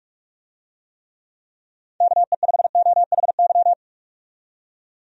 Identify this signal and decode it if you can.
Morse code, heard as a keyed tone, about 46 wpm, 700 Hz. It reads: KE5OHY
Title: USA Call Signs - 6 Characters - 45wpm